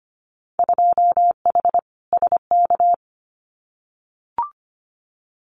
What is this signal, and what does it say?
Morse code, heard as a keyed tone, about 25 wpm, 700 Hz. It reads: 2H SK
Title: Winter Field Day Exchanges Encore 25wpm